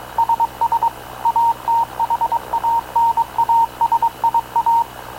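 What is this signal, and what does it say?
Morse code, heard as a keyed tone, about 22 wpm, 920 Hz, with radio noise. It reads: SS ATHANASIA
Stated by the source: Havana Radio / CLA sending Spanish press (PX) in Morse, recorded by Radio Officer Paul Dumesnil onboard CNAV QUEST/CZDO.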